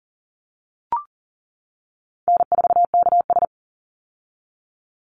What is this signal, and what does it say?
Morse code, heard as a keyed tone, about 40 wpm, 700 Hz. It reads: N4KS
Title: Easy Call Signs - 40wpm